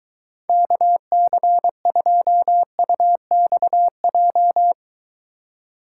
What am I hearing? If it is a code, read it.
Morse code, heard as a keyed tone, about 23 wpm, 700 Hz. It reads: KC2UXJ